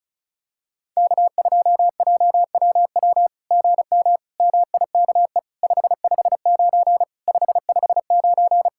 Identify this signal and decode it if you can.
Morse code, heard as a keyed tone, about 33 wpm, 700 Hz. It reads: K2JWW GM MIKE 559 559
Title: SOTA QSO Practice 35wpm